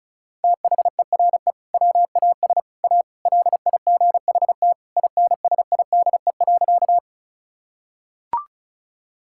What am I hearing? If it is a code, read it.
Morse code, heard as a keyed tone, about 36 wpm, 700 Hz. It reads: THERE WAS A LIGHT INSIDE.
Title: Sentences from Top 1000 Words - Review Part 6 - 35wpm